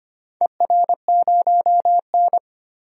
Morse code, heard as a keyed tone, about 25 wpm, 700 Hz. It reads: ER0N
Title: DXCC-100 Call Signs - 4 Characters - 25wpm